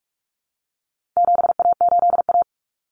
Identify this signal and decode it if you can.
Morse code, heard as a keyed tone, about 44 wpm, 700 Hz. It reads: 7A8A
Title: Common Cut Numbers - 4-Digit 45wpm